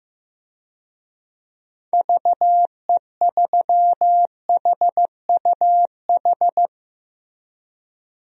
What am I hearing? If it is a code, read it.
Morse code, heard as a keyed tone, about 15 wpm, 700 Hz. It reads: VE3HUH
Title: DXCC-100 Call Signs - 6 Characters - Repeated 3x - 15wpm